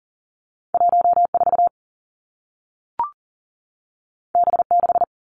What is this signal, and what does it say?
Morse code, heard as a keyed tone, about 40 wpm, 700 Hz. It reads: 1 4 B 6
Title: Two Challenging Letter-Number Combinations 40wpm